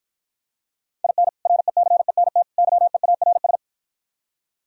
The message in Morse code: IN RECENT YEARS

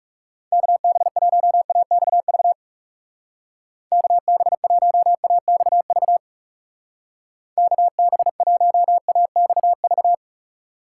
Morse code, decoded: KB1AXV KB1AXV KB1AXV

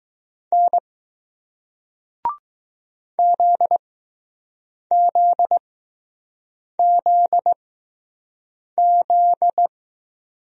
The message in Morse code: N ZZZZ

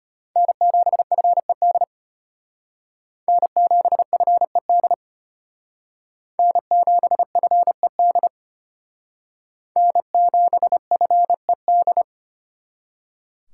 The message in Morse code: N7FED N7FED N7FED N7FED

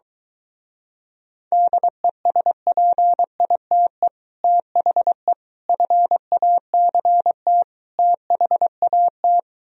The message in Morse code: DESPITE THE FACT THAT

